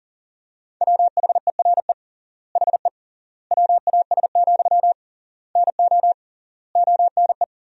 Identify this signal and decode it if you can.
Morse code, heard as a keyed tone, about 40 wpm, 700 Hz. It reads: WHERE HE WAS, NO ONE
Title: 3x Word-Spacing - Sentences from Top 400 Words - 40wpm